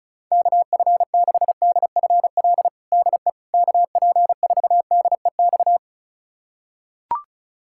Morse code, heard as a keyed tone, about 36 wpm, 700 Hz. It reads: KF6DFL DE KP4DEX
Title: USA Callsign DE Callsign - 6 Characters - 35wpm